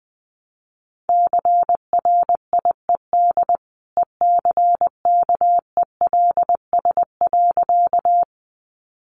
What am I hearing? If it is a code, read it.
Morse code, heard as a keyed tone, about 21 wpm, 700 Hz. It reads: CRIED ECKELS.
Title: A Sound Of Thunder 20wpm